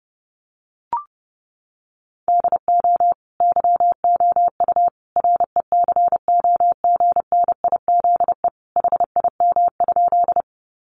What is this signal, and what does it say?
Morse code, heard as a keyed tone, about 30 wpm, 700 Hz. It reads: DO YOU RECOGNIZE HIM?
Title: Sentences from Top 1400 Words 30wpm